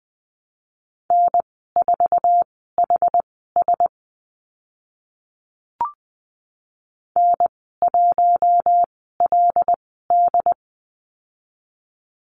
Morse code, heard as a keyed tone, about 20 wpm, spaced as for 15 wpm, 700 Hz. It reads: N4HS N1LD
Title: 082 Lesson 14g - Number 2 - Call Signs 20wpm